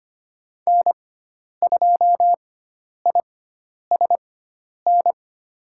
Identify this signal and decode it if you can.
Morse code, heard as a keyed tone, about 25 wpm, spaced as for 10 wpm, 700 Hz. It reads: N2ISN
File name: Lesson 16g - Letter c - Call Signs 25wpm